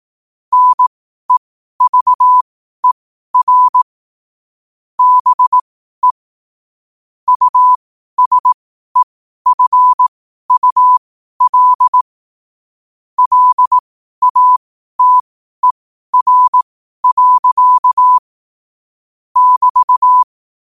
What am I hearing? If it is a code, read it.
Morse code, heard as a keyed tone, about 18 wpm, spaced as for 12 wpm, 1000 Hz. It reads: NEVER BE USEFUL LATER. <BT>
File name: qotdfiles / 12wpm